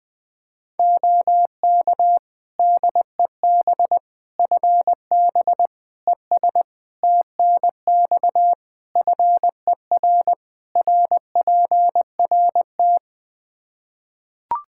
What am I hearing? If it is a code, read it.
Morse code, heard as a keyed tone, about 20 wpm, 700 Hz. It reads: OK DEB FB ES TNX FER RPRT
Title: Practice QSO - Exchange 2 - Info - 20wpm